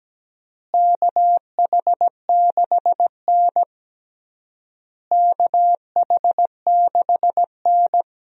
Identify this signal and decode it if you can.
Morse code, heard as a keyed tone, about 17 wpm, 700 Hz. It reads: KH6N KH6N
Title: DXCC-100 Call Signs - 4 Characters - Repeated 3x - 17wpm